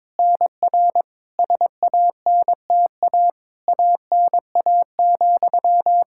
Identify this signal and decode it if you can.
Morse code, heard as a keyed tone, about 21 wpm, 700 Hz. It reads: NR SANTA ANA,
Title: Practice QSO - Exchange 1 - Beginning - 22wpm